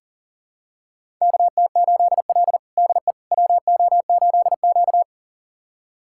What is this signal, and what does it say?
Morse code, heard as a keyed tone, about 40 wpm, 700 Hz. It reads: KT8L DE WO8Q
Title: USA Callsign DE Callsign - 4 Characters - 40wpm